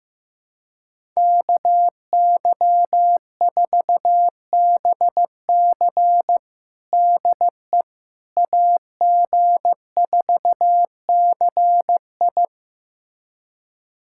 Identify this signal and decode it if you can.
Morse code, heard as a keyed tone, about 15 wpm, 700 Hz. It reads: KY4BC DE AG4CI
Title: USA Callsign DE Callsign - 5 Characters - 15wpm